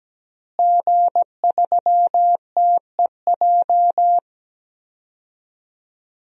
Morse code, heard as a keyed tone, about 17 wpm, 700 Hz. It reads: G3TEJ
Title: DXCC-100 Call Signs - 5 Characters - Repeated 3x - 17wpm